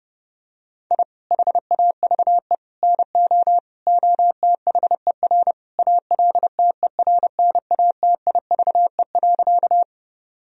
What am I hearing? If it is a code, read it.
Morse code, heard as a keyed tone, about 30 wpm, 700 Hz. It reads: I HAVE NO OTHER ALTERNATIVE.
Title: Sentences from Top 1000 Words - Review Part 4 - 30wpm